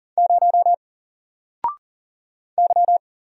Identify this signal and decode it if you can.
Morse code, heard as a keyed tone, about 40 wpm, 700 Hz. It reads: 0 Y
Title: Single Character (Letter:Number:Punctuation:Prosign) Alt 40wpm